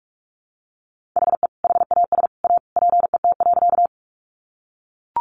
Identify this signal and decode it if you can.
Morse code, heard as a keyed tone, about 44 wpm, 700 Hz. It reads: HE HAS A PET.
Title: Easy Phrases 45wpm